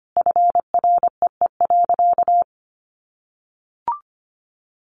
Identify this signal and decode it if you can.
Morse code, heard as a keyed tone, about 25 wpm, 700 Hz. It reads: FREE.
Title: Sentences from Top 500 Words 25wpm